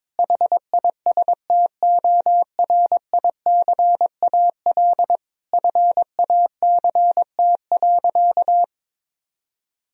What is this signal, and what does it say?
Morse code, heard as a keyed tone, about 22 wpm, 700 Hz. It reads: HISTORICAL FACT.
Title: Sentences from Top 1000 Words - Review Part 4 - 22wpm